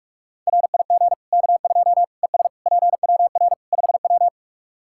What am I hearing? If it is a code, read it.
Morse code, heard as a keyed tone, about 44 wpm, 700 Hz. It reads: RIG K2 ES PWR 5W
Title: Practice QSO - Exchange 2 - Info - 45wpm